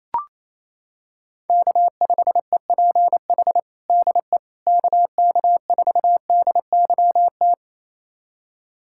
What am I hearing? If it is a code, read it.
Morse code, heard as a keyed tone, about 28 wpm, 700 Hz. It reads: K5EPH DE KK4DYT
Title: USA Callsign DE Callsign 28wpm